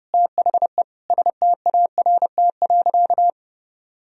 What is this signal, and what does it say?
Morse code, heard as a keyed tone, about 32 wpm, 700 Hz. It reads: THE START.
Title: Sentences from Top 1000 Words - Review Part 1 - 30wpm